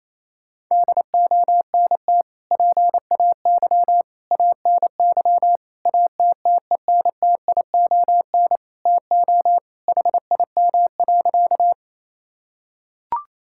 Morse code, heard as a keyed tone, about 28 wpm, 700 Hz. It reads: DONT PAY ANY ATTENTION TO HIM.
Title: Sentences from Top 1000 Words - Review Part 6 - 28wpm